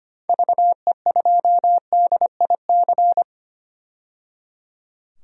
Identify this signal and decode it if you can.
Morse code, heard as a keyed tone, about 25 wpm, 690 Hz. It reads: VE2DIC
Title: DXCC-100 Call Signs - 6 Characters - 25wpm